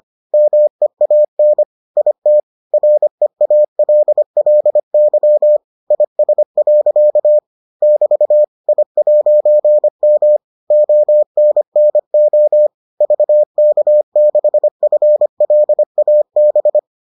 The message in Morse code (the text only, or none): MEAN IT REALLY IS. <BT> I'M ONNO VK6FLAB